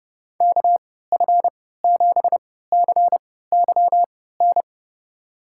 Morse code, heard as a keyed tone, about 32 wpm, spaced as for 17 wpm, 700 Hz. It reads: KF7CYN
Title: USA Call Signs - 6 Characters - 3x Character Spacing - 30wpm